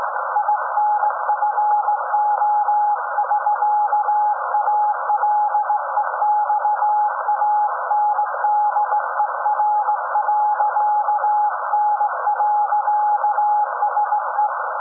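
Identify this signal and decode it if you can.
Morse code, heard as a keyed tone, about 18 wpm, 840 Hz, with noise. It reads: INFO <SK> ALEXANDER.N.SE
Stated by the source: SAQ Grimeton Radio, Alexanderson 200 kW alternator on 17.2 kHz, received in Potsdam with a very strong signal on World Radio Day.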